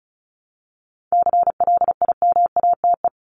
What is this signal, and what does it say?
Morse code, heard as a keyed tone, about 34 wpm, 700 Hz. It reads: CLIMATE
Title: Top 1501-1750 Words 35wpm